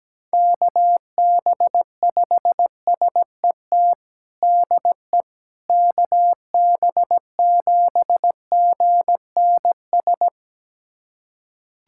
Morse code, heard as a keyed tone, about 17 wpm, 700 Hz. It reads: KB5SET DE KB7GNS
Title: USA Callsign DE Callsign - 6 Characters - 17wpm